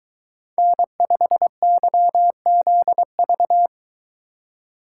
Morse code, heard as a keyed tone, about 22 wpm, 700 Hz. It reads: N5YZV